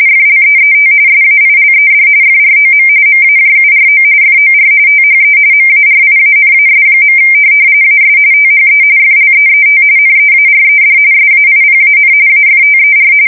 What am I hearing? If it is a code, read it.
RTTY, about 45.45 baud. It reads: FLEET SAILED FROM MAJURO FOR THE MARIANAS. AS THE FLEET NEARED SAIPAN AFTER
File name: fsk170 / USS-New-Jersey